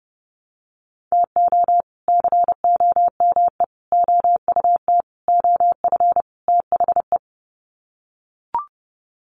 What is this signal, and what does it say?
Morse code, heard as a keyed tone, about 28 wpm, 700 Hz. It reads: TO COME OUT OF THE
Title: Sets of 5 Words Encore2 30wpm